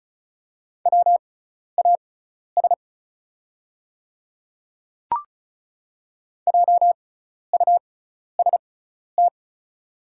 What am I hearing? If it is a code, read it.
Morse code, heard as a keyed tone, about 36 wpm, spaced as for 12 wpm, 690 Hz. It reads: WAS JUST
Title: 6x Character Spacing - Top 300 Words - 35wpm